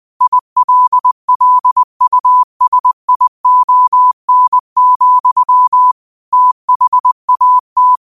Morse code, heard as a keyed tone, about 20 wpm, 1000 Hz. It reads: ILLUSION, THAT